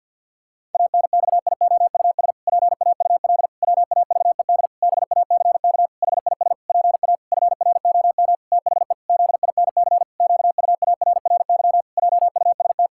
Morse code, heard as a keyed tone, about 47 wpm, 700 Hz. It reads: ANXIOUS PAUL WAVED BACK HIS PA FROM THE ZINC QUARRY JUST
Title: Pangram Challenge v2 - 50wpm